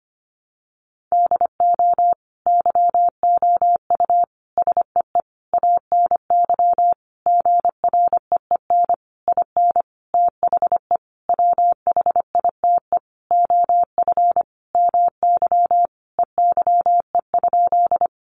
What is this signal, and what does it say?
Morse code, heard as a keyed tone, about 25 wpm, 700 Hz. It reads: DO YOU SEE ANY GREEN IN THE WHITE OF MY EYE?
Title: Sentences from Top 500 Words 25wpm